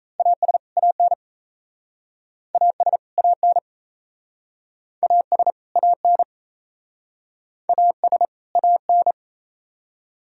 Morse code, heard as a keyed tone, about 37 wpm, 700 Hz. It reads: AS AN AS AN AS AN AS AN